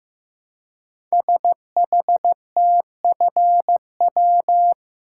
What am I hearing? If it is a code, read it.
Morse code, heard as a keyed tone, about 15 wpm, 700 Hz. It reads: SHTFW
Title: Code Groups - 5 Letters - 15wpm